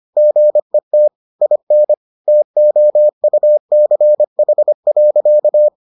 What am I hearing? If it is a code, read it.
Morse code, heard as a keyed tone, about 25 wpm, 600 Hz. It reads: GET IN TOUCH.